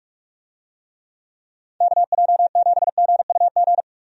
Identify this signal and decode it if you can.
Morse code, heard as a keyed tone, about 46 wpm, 700 Hz. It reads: KJ7GUG
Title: USA Call Signs - 6 Characters - 45wpm